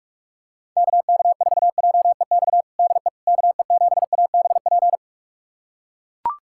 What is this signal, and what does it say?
Morse code, heard as a keyed tone, about 41 wpm, 700 Hz. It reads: KK4JEX DE KE7ABP
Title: USA Callsign DE Callsign - 6 Characters - 45wpm